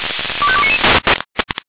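Five-tone selective call (ZVEI), heard as quick runs of five tones, tones of about 70 ms. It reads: 25299
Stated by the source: Five-tone ZVEI selective call received by SDR.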